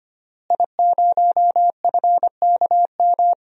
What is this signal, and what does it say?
Morse code, heard as a keyed tone, about 25 wpm, 700 Hz. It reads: I0FKM
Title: DXCC-100 Call Signs - 5 Characters - Repeated 3x - 25wpm